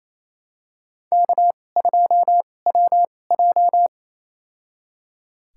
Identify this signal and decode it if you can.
Morse code, heard as a keyed tone, about 28 wpm, spaced as for 20 wpm, 700 Hz. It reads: K2WJ
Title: USA Call Signs - 4 Characters - 2x Character Spacing - 28wpm